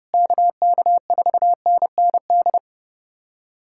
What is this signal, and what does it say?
Morse code, heard as a keyed tone, about 32 wpm, 700 Hz. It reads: KK4NND
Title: USA Callsign DE Callsign 30wpm